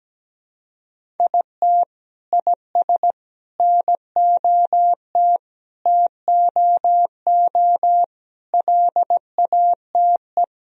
Morse code, heard as a keyed tone, about 17 wpm, 700 Hz. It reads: IT IS NOT TOO LATE
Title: Sentences from Top 1000 Words - Review Part 1 - 17wpm